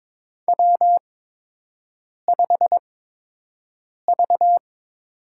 Morse code, heard as a keyed tone, about 23 wpm, spaced as for 6 wpm, 700 Hz. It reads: W5V
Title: USA Call Signs - 5 Characters - 8x Character Spacing - 22wpm